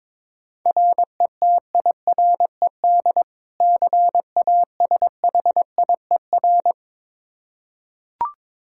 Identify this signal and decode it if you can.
Morse code, heard as a keyed tone, about 22 wpm, 700 Hz. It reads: RETIRED CASHIER
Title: Practice QSO - Exchange 3 - More Info - 22wpm